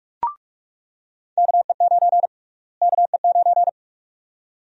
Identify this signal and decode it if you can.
Morse code, heard as a keyed tone, about 46 wpm, 700 Hz. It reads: KE9 KE9
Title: USA Call Sign Prefixes - Repeated 2x - 45wpm